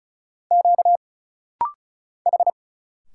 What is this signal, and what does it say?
Morse code, heard as a keyed tone, about 36 wpm, 700 Hz. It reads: Q H
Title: Single Letter-Number Alt - Rapid-Fire 35wpm